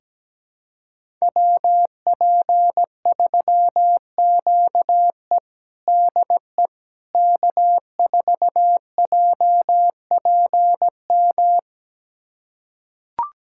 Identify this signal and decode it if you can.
Morse code, heard as a keyed tone, about 17 wpm, 700 Hz. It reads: WP3QE DE K4JPM
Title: USA Callsign DE Callsign - 5 Characters - 17wpm